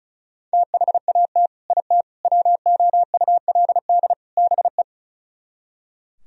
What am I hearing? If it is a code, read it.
Morse code, heard as a keyed tone, about 46 wpm, spaced as for 41 wpm, 700 Hz. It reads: THAT IT WOULD BE
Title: Speed-Racing - Sets of 4 Words 35wpm